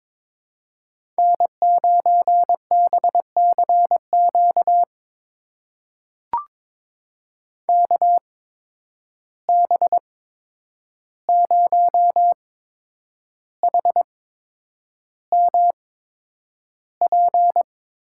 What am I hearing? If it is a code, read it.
Morse code, heard as a keyed tone, about 21 wpm, spaced as for 9 wpm, 700 Hz. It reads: N9BCQ KB0HMP
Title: USA Call Signs - 8x Character Spacing - 22wpm